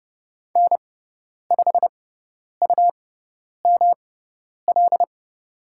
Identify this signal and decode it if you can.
Morse code, heard as a keyed tone, about 30 wpm, spaced as for 10 wpm, 700 Hz. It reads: N5UML
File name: Lesson 18g - Letter m - Call Signs 30wpm